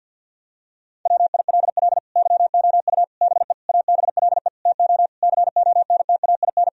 Morse code, heard as a keyed tone, about 52 wpm, 700 Hz. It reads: WILL YOU BE ABLE TO CONTAIN
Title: Sentences from Top 1000 Words - Review Part 2 - 50wpm